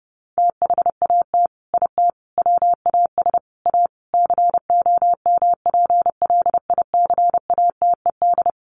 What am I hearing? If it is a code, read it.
Morse code, heard as a keyed tone, about 30 wpm, 700 Hz. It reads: THAT IT WAS A COMPLICATED
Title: Sentences from Top 1200 Words 30wpm